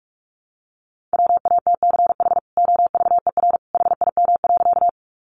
Morse code, heard as a keyed tone, about 44 wpm, 700 Hz. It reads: WATCH OVER HIM.
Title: Sentences from Top 300 Words 45wpm